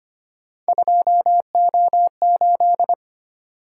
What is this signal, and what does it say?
Morse code, heard as a keyed tone, about 25 wpm, 700 Hz. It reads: 2O8
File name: Three Letter-Number Combinations - 25wpm